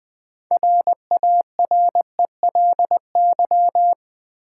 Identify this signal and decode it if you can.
Morse code, heard as a keyed tone, about 20 wpm, 700 Hz. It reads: RARELY
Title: Sentences from Top 1400 Words - New Word Drill - 20wpm